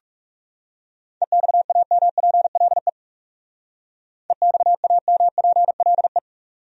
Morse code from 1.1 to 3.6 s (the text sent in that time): EXAMPLE